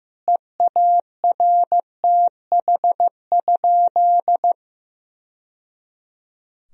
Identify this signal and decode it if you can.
Morse code, heard as a keyed tone, about 15 wpm, 700 Hz. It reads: EARTH?
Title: Sentences from Top 1000 Words - Review Part 3 - 15wpm